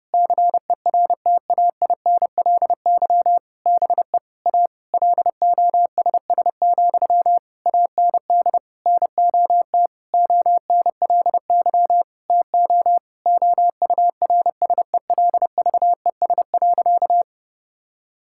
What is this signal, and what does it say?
Morse code, heard as a keyed tone, about 30 wpm, 700 Hz. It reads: CERTAINLY BE A LOSS, AND NOT ONLY TO OURSELVES.
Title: Sentences from Top 1600 Words 30wpm